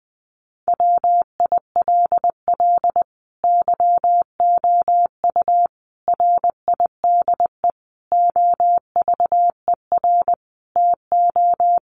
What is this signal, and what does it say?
Morse code, heard as a keyed tone, about 20 wpm, 700 Hz. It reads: WILL YOU RIDE OVER TO
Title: Sentences from Top 1200 Words 20wpm